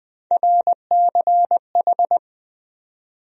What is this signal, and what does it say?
Morse code, heard as a keyed tone, about 20 wpm, 700 Hz. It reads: RCH
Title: USA Call Sign Postfixes - 3 Letters 20wpm